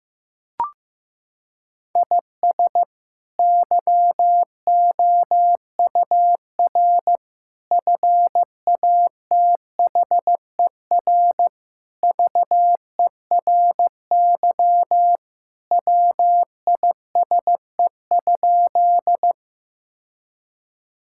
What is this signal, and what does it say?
Morse code, heard as a keyed tone, about 16 wpm, 700 Hz. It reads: IS YOUR FATHER VERY WISE?
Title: Sentences from Top 1600 Words 15wpm